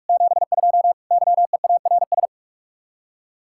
Morse code, heard as a keyed tone, about 43 wpm, 700 Hz. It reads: 72 YEARS